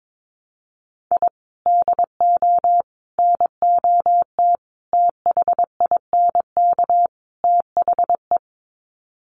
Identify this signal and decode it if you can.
Morse code, heard as a keyed tone, about 22 wpm, 700 Hz. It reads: I DO NOT THINK THE
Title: Sets of 5 Words 22wpm